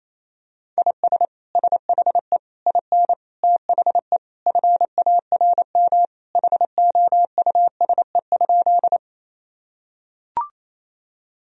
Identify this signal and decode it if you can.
Morse code, heard as a keyed tone, about 28 wpm, 700 Hz. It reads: IS SHE IN THE FARM HOUSE?
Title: Sentences from Top 1000 Words - Review Part 2 - 28wpm